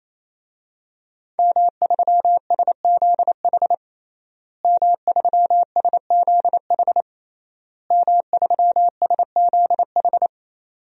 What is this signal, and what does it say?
Morse code, heard as a keyed tone, about 27 wpm, 700 Hz. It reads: M3SZH M3SZH M3SZH
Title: DXCC-100 Call Signs - 5 Characters - Repeated 3x - 28wpm